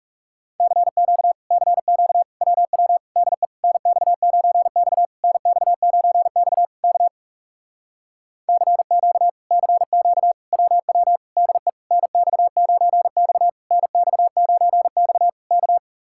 Morse code, heard as a keyed tone, about 40 wpm, 700 Hz. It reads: CQ CQ WW DE NX9X NX9X K CQ CQ WW DE NX9X NX9X K